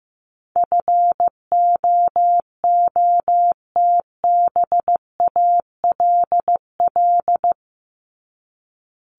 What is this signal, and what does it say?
Morse code, heard as a keyed tone, about 15 wpm, 700 Hz. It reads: FOOTBALL
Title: Top 1501-1750 Words 15wpm